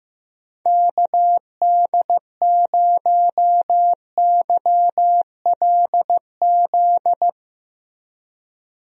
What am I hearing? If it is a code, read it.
Morse code, heard as a keyed tone, about 15 wpm, 700 Hz. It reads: KD0YLZ
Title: USA Call Signs - 6 Characters - Repeated 2x - 15wpm